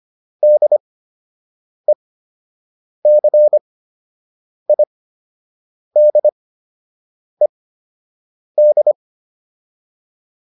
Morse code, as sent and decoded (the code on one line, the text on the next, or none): -.. -.-. .. -.. -..
DECIDED